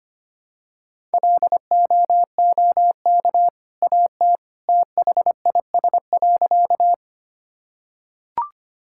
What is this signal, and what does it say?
Morse code, heard as a keyed tone, about 25 wpm, 700 Hz. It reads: LOOK AT THIS.
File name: practice / 25wpm / Sentences with 2-Word Phrases - 25wpm